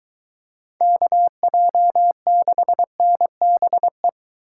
Morse code, heard as a keyed tone, about 22 wpm, 700 Hz. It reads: KJ6NBE